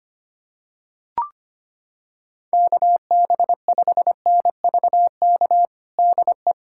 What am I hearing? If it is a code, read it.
Morse code, heard as a keyed tone, about 25 wpm, 700 Hz. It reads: KB5NVK DE
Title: USA Callsign DE Callsign 25wpm